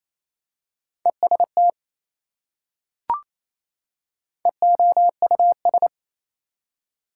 Morse code, heard as a keyed tone, about 27 wpm, 700 Hz. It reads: EST EOUS
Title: Prefixes and Suffixes 28wpm